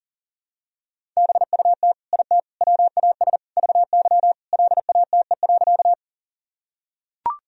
Morse code, heard as a keyed tone, about 37 wpm, 700 Hz. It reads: BUT IT WAS VY LATE.
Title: Sentences from Top 300 Words and Abbreviations 40wpm